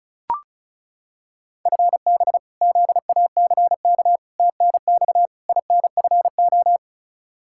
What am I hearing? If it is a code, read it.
Morse code, heard as a keyed tone, about 36 wpm, 700 Hz. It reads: FB ZACK TNX INFO
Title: Practice QSO - Exchange 3 - More Info - 35wpm